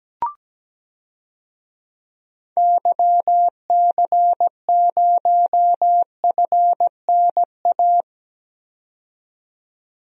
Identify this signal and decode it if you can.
Morse code, heard as a keyed tone, about 17 wpm, 700 Hz. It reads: YC0FNA
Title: DXCC-100 Call Signs - 6 Characters - 17wpm